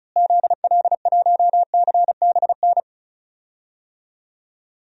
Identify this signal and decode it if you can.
Morse code, heard as a keyed tone, about 34 wpm, 700 Hz. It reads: ZL1CBN
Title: DXCC-100 Call Signs - 6 Characters - Repeated 3x - 35wpm